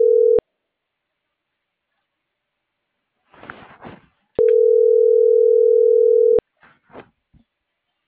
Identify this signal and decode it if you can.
Ringback tone (North America), 440 + 480 Hz.